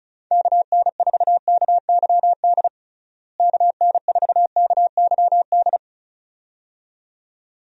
Morse code, heard as a keyed tone, about 36 wpm, 700 Hz. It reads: KN4KYD KN4KYD
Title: USA Call Signs - 6 Characters - Repeated 2x - 35wpm